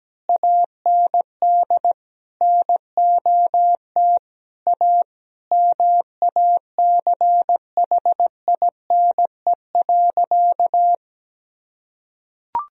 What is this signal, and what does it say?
Morse code, heard as a keyed tone, about 17 wpm, 700 Hz. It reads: AND NOT A MACHINE.
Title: Sentences from Top 1000 Words - Review Part 2 - 17wpm